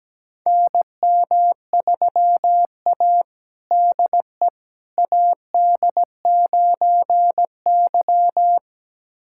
Morse code, heard as a keyed tone, about 17 wpm, 700 Hz. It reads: NM3A DE AD9Y
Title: USA Callsign DE Callsign - 4 Characters - 17wpm